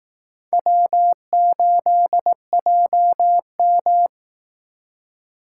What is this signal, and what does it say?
Morse code, heard as a keyed tone, about 18 wpm, 700 Hz. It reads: W8JM
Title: Speed Racing - USA 1x2 Call Signs 15wpm